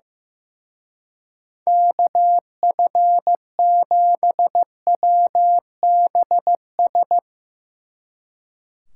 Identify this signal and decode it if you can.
Morse code, heard as a keyed tone, about 16 wpm, 700 Hz. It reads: KF7WBS
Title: Speed Racing - USA 2x3 Call Signs 15wpm